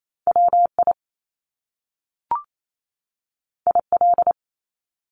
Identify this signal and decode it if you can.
Morse code, heard as a keyed tone, about 28 wpm, 700 Hz. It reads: WI IL
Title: US States 28wpm